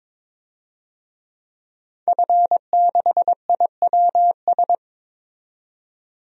Morse code, heard as a keyed tone, about 21 wpm, 700 Hz. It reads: F6IWS
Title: DXCC-100 Call Signs - 5 Characters - 22wpm